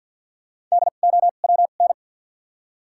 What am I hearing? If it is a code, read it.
Morse code, heard as a keyed tone, about 53 wpm, 700 Hz. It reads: D O W N
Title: 2x Character Spacing - Top 100 Words - 50wpm